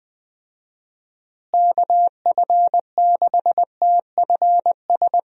Morse code, heard as a keyed tone, about 21 wpm, 700 Hz. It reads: KF6TFS
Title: USA Call Signs - 6 Characters - Repeated 2x - 20wpm